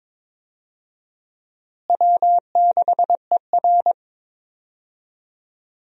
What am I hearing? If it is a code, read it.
Morse code, heard as a keyed tone, about 22 wpm, 700 Hz. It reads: W6ER
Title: USA Call Signs - 4 Characters - Repeated 2x - 22wpm